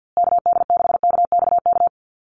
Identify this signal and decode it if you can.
Morse code, heard as a keyed tone, about 50 wpm, 700 Hz. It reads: KD6KXK
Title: USA Call Signs Repeated 2x 50wpm